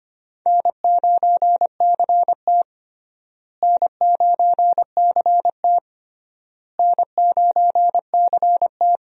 Morse code, heard as a keyed tone, about 25 wpm, 700 Hz. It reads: N9CT N9CT N9CT
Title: USA Call Signs - 4 Characters - Repeated 3x - 25wpm